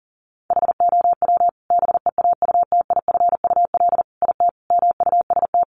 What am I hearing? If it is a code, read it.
Morse code, heard as a keyed tone, about 40 wpm, 700 Hz. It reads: HOW BEAUTIFUL IT MUST
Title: Sentences from Top 900 Words 40wpm